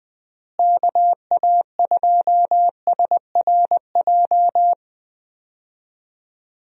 Morse code, heard as a keyed tone, about 20 wpm, 700 Hz. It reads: KA2SRJ
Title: USA Call Signs - 6 Characters - Repeated 3x - 20wpm